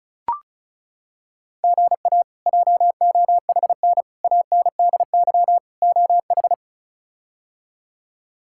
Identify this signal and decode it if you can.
Morse code, heard as a keyed tone, about 36 wpm, 700 Hz. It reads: GA JOHN ANDY OH
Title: K1USN SST QSO Practice - 35wpm